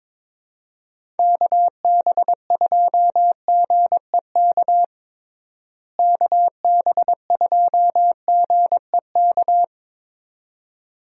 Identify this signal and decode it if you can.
Morse code, heard as a keyed tone, about 22 wpm, 700 Hz. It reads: KB2GEK KB2GEK
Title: USA Call Signs - 6 Characters - Repeated 2x - 22wpm